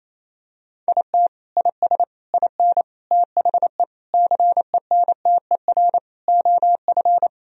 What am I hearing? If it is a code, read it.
Morse code, heard as a keyed tone, about 28 wpm, 690 Hz. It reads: IT IS IN THE CENTER OF
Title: Sentences from Top 1000 Words - Review Part 2 - 28wpm